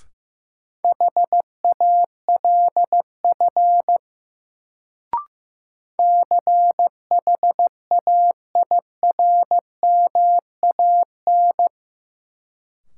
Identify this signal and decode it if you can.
Morse code, heard as a keyed tone, about 15 wpm, 700 Hz. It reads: HALF CHAIRMAN
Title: Sets of 3 Words - New Word Drill - 15wpm